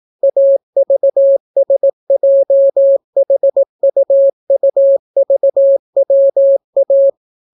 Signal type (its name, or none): Morse code